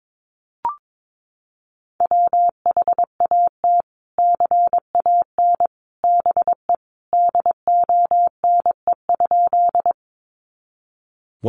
Morse code, heard as a keyed tone, about 22 wpm, 700 Hz. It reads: WHAT CAN BE DONE?
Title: Sentences from Top 500 Words 22wpm